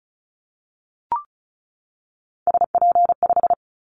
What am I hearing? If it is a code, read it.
Morse code, heard as a keyed tone, about 36 wpm, 700 Hz. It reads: SP5
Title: Callsign Prefixes for Top 25 DXCC - 35wpm